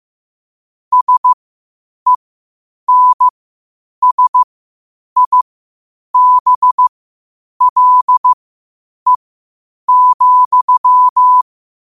Morse code, heard as a keyed tone, about 15 wpm, spaced as for 8 wpm, 1000 Hz. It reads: SENSIBLE,
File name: qotdfiles / 8wpm